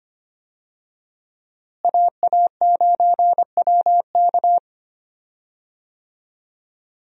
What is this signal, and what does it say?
Morse code, heard as a keyed tone, about 25 wpm, 700 Hz. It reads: AA9WK
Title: USA Call Signs - 5 Characters - Repeated 3x - 25wpm